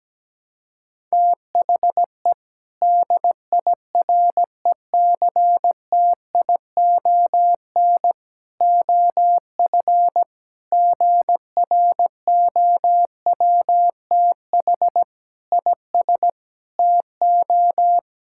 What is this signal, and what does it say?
Morse code, heard as a keyed tone, about 17 wpm, 700 Hz. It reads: THE DIRECTION OF GROWTH IS TO
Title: Sentences from Top 1000 Words - Review Part 5 - 17wpm